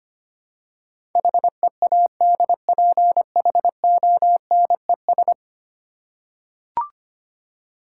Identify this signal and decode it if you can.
Morse code, heard as a keyed tone, about 25 wpm, 700 Hz. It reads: HEADPHONES
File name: practice / 25wpm / Easy Words with Hints - 25wpm